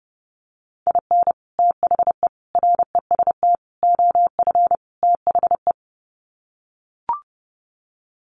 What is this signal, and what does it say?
Morse code, heard as a keyed tone, about 28 wpm, 700 Hz. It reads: IN THE REST OF THE
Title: Sets of 5 Words 30wpm